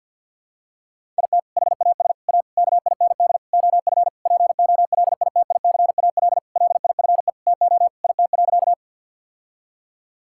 Morse code, heard as a keyed tone, about 50 wpm, 700 Hz. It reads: IT HAS A KIND OF POLITICAL LIFE TO IT.
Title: Sentences from Top 1000 Words - Review Part 5 - 50wpm